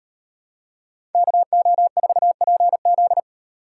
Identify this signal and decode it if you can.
Morse code, heard as a keyed tone, about 38 wpm, 700 Hz. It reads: KO4PZ